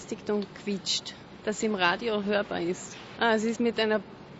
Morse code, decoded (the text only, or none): none